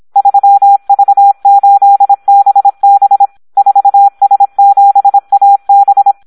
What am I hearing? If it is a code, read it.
Morse code, heard as a keyed tone, about 26 wpm, 800 Hz. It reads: 3V8BB 4S7AB
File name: Audio practica CW / SD 1 26